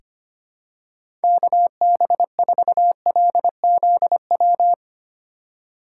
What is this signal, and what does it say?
Morse code, heard as a keyed tone, about 25 wpm, 700 Hz. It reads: KB4LZW
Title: USA Call Signs - 6 Characters - 3x Character Spacing - 25wpm